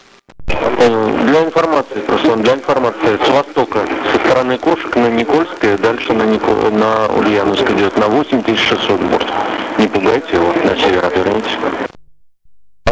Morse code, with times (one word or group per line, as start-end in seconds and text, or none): none